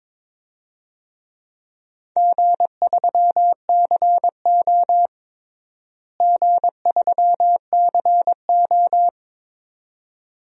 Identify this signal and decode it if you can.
Morse code, heard as a keyed tone, about 22 wpm, 700 Hz. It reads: G3CO G3CO
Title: DXCC-100 Call Signs - 4 Characters - Repeated 2x - 22wpm